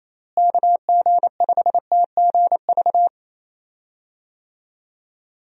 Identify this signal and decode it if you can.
Morse code, heard as a keyed tone, about 27 wpm, 700 Hz. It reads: KG5TGV
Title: USA Call Signs - 6 Characters - Repeated 3x - 28wpm